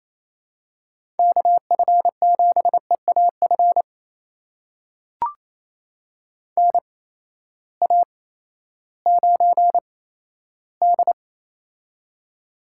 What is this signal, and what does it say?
Morse code, heard as a keyed tone, about 28 wpm, spaced as for 14 wpm, 700 Hz. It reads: KF7EAF NA9D
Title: USA Call Signs - 8x Character Spacing - 28wpm